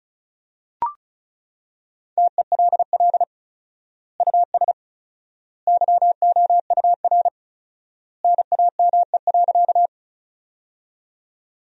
Morse code, heard as a keyed tone, about 36 wpm, 700 Hz. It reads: TELL US YOUR NAME.
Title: 4x Word-Spacing - Sentences from Top 200 Words - 35wpm